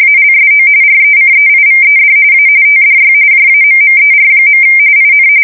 RTTY, about 45.45 baud. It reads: ALSO RECEIVED A TDY JAMMER. IN
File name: fsk170 / USS-Massachusetts